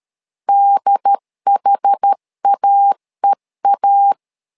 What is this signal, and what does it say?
Morse code, heard as a keyed tone, about 13 wpm, 800 Hz. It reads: DHAEA
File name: LEZIONI / 2 UHDN / 2 UHD 12wpm 1540